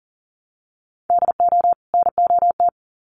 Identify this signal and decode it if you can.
Morse code, heard as a keyed tone, about 40 wpm, 700 Hz. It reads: DO NOT
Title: Sets of 2 Words 40wpm